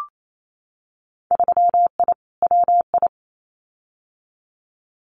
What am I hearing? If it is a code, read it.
Morse code, heard as a keyed tone, about 28 wpm, 700 Hz. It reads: 3I WI
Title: Winter Field Day Exchanges 28wpm